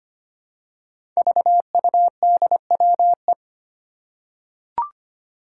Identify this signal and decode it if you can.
Morse code, heard as a keyed tone, about 25 wpm, 700 Hz. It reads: VUDWE
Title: Five Challenging Letter-Number Combinations 25wpm